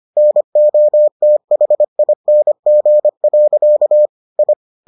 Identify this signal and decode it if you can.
Morse code, heard as a keyed tone, about 25 wpm, 600 Hz. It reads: NOTHING. I